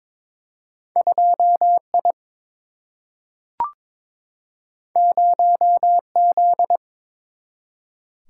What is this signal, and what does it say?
Morse code, heard as a keyed tone, about 22 wpm, 700 Hz. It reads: 2I 0Z
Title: Single-Letter Single-Number 22wpm